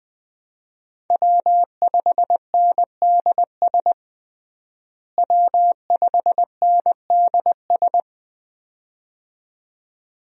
Morse code, heard as a keyed tone, about 20 wpm, 700 Hz. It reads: W5NDS W5NDS
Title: USA Call Signs - 5 Characters - Repeated 2x - 20wpm